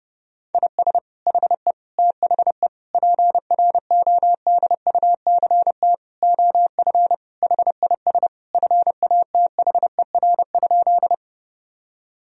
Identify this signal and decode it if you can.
Morse code, heard as a keyed tone, about 32 wpm, 700 Hz. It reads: IS HE THE PRODUCT OF HIS FATHER?
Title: Sentences from Top 1000 Words - Review Part 2 - 30wpm